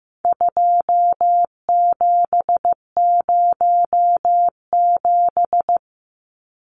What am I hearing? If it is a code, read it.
Morse code, heard as a keyed tone, about 15 wpm, 700 Hz. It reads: 2707
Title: Numbers - 4-Digit 15wpm